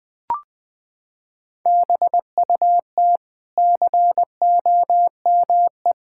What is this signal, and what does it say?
Morse code, heard as a keyed tone, about 20 wpm, 700 Hz. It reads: BUT COME
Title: Sentences from Top 1000 Words - Review Part 6 - 20wpm